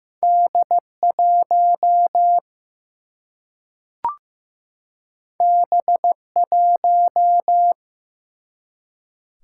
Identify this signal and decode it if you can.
Morse code, heard as a keyed tone, about 15 wpm, 700 Hz. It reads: D 1 B 1
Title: Single-Letter Single-Number 15wpm